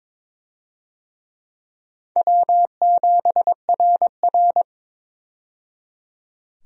Morse code, heard as a keyed tone, about 22 wpm, 700 Hz. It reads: W7RR
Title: USA Call Signs - 4 Characters - 22wpm